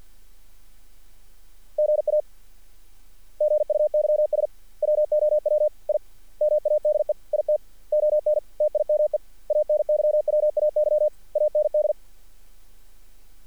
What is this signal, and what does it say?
Morse code, heard as a keyed tone, about 48 wpm, 600 Hz. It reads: GA GUYS WOW I MADE IT ON TIME ANYWAY AND
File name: archived / CRSnet